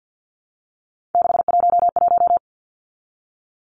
Morse code, heard as a keyed tone, about 50 wpm, 700 Hz. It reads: 611
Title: Numbers - 3-Digit 50wpm